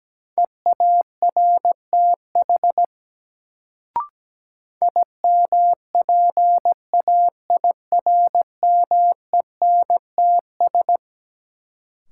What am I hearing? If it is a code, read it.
Morse code, heard as a keyed tone, about 17 wpm, 700 Hz. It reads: EARTH IMPAIRMENTS
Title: Sets of 5 Words Encore - New Word Drill - 17wpm